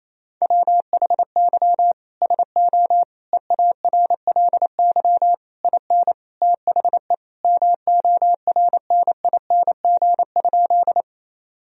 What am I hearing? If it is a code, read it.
Morse code, heard as a keyed tone, about 28 wpm, 700 Hz. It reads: WHY SO EARLY IN THE MORNING?
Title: Sentences from Top 1000 Words - Review Part 2 - 28wpm